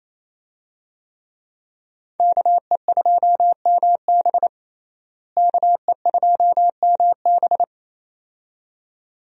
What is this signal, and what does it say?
Morse code, heard as a keyed tone, about 28 wpm, 700 Hz. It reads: KE2MB KE2MB
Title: USA Call Signs - 5 Characters - Repeated 2x - 28wpm